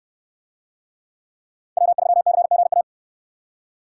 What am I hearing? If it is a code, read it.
Morse code, heard as a keyed tone, about 69 wpm, 700 Hz. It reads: W3CGA